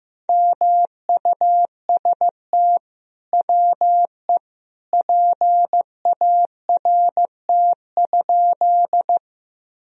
Morse code, heard as a keyed tone, about 15 wpm, 700 Hz. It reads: MUST WE PART?
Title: Sentences from Top 1000 Words - Review Part 1 - 15wpm